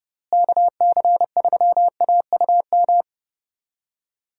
Morse code, heard as a keyed tone, about 31 wpm, 700 Hz. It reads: KC3AUM
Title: USA Call Signs - 6 Characters - Repeated 2x - 30wpm